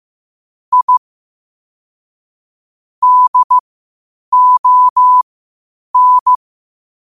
Morse code, heard as a keyed tone, about 15 wpm, spaced as for 8 wpm, 1000 Hz. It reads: I DON
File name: qotdfiles / 8wpm